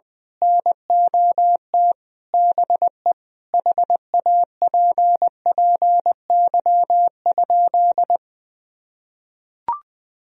Morse code, heard as a keyed tone, about 20 wpm, 700 Hz. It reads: NOT BE HAPPY?
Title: Sentences from Top 1000 Words - Review Part 2 - 20wpm